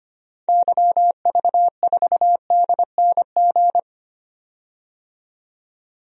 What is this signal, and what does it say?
Morse code, heard as a keyed tone, about 25 wpm, 700 Hz. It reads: YV4DNG
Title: DXCC-100 Call Signs - 6 Characters - 25wpm